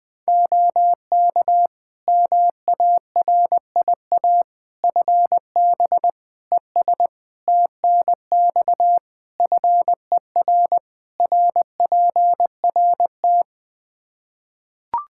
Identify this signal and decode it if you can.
Morse code, heard as a keyed tone, about 20 wpm, 700 Hz. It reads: OK MARIA FB ES TNX FER RPRT
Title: Practice QSO - Exchange 2 - Info - 20wpm